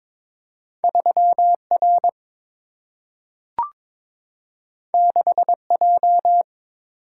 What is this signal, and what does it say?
Morse code, heard as a keyed tone, about 22 wpm, 700 Hz. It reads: 3R 6J
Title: Single-Letter Single-Number 22wpm